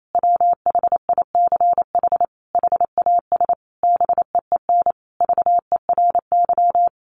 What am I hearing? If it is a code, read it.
Morse code, heard as a keyed tone, about 28 wpm, 700 Hz. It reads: WHICH HAS BEEN VERY
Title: Sentences from Top 1400 Words 28wpm